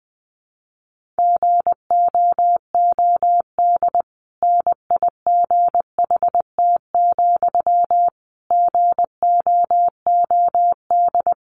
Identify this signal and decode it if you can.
Morse code, heard as a keyed tone, about 20 wpm, 700 Hz. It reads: GOOD NIGHT, GOOD
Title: Sentences from Top 1500 Words 20wpm